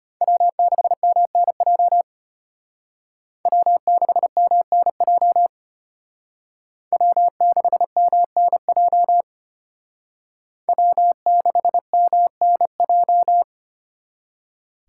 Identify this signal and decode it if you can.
Morse code, heard as a keyed tone, about 32 wpm, 700 Hz. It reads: W6MNJ W6MNJ W6MNJ W6MNJ